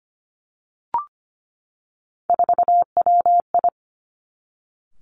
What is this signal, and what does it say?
Morse code, heard as a keyed tone, about 25 wpm, 700 Hz. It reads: 4WI
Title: Three Challenging Letter-Number Combinations 25wpm